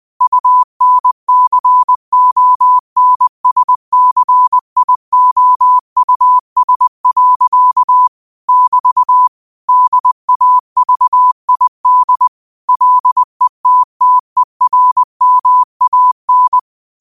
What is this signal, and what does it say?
Morse code, heard as a keyed tone, about 20 wpm, 1000 Hz. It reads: UNCONSCIOUS. <BT> DAVID LETTERMAN
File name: qotdfiles / 20wpm